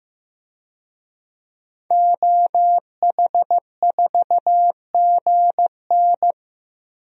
Morse code, heard as a keyed tone, about 16 wpm, 700 Hz. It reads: OH4GN
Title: DXCC-100 Call Signs - 5 Characters - Repeated 3x - 15wpm